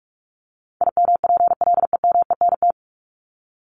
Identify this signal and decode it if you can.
Morse code, heard as a keyed tone, about 44 wpm, 700 Hz. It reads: IMPLEMENT
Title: Top 1501-1750 Words 45wpm